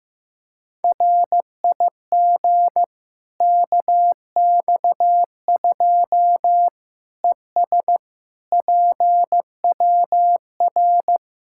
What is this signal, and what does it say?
Morse code, heard as a keyed tone, about 15 wpm, 700 Hz. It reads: RIG KX2 ES PWR
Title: Practice QSO - Exchange 2 - Info - 15wpm